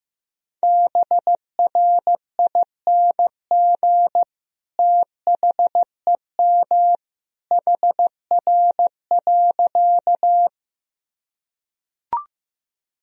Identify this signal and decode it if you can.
Morse code, heard as a keyed tone, about 15 wpm, 700 Hz. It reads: BRING THEM HR.
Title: Sentences from Top 500 Words and Abbreviations 15wpm